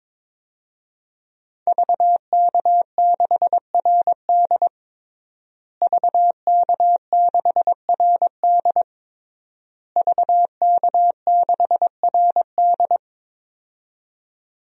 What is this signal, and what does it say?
Morse code, heard as a keyed tone, about 22 wpm, 700 Hz. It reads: VK6RD VK6RD VK6RD
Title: DXCC-100 Call Signs - 5 Characters - Repeated 3x - 22wpm